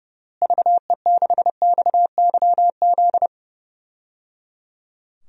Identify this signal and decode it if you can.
Morse code, heard as a keyed tone, about 30 wpm, 700 Hz. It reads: VE6XYZ
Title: DXCC-100 Call Signs - 6 Characters - Repeated 3x - 30wpm